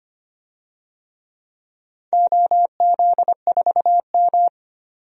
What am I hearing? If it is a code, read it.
Morse code, heard as a keyed tone, about 25 wpm, 700 Hz. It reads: OZ4M
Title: DXCC-100 Call Signs - 4 Characters - 25wpm